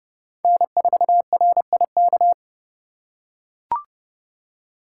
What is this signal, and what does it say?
Morse code, heard as a keyed tone, about 30 wpm, 700 Hz. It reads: N4RIK
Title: USA Call Signs - 5 Characters - 2x Character Spacing - 30wpm